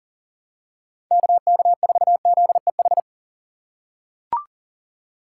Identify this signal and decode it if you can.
Morse code, heard as a keyed tone, about 38 wpm, 700 Hz. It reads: KK4ZEH
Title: USA Call Signs - 6 Characters - 6x Character Spacing - 40wpm